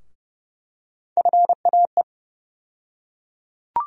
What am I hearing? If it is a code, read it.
Morse code, heard as a keyed tone, about 30 wpm, 700 Hz. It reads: FAE
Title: USA Call Sign Postfixes - 3 Letters 30wpm